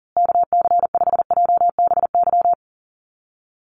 Morse code, heard as a keyed tone, about 37 wpm, 700 Hz. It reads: KC5JBY
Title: USA Call Signs Repeated 3x 40wpm